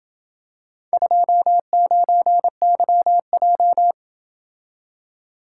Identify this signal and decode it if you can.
Morse code, heard as a keyed tone, about 28 wpm, 700 Hz. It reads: 29YJ